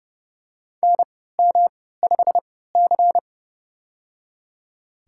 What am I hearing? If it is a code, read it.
Morse code, heard as a keyed tone, about 32 wpm, spaced as for 17 wpm, 700 Hz. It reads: NM5C
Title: USA Call Signs - 4 Characters - 3x Character Spacing - 30wpm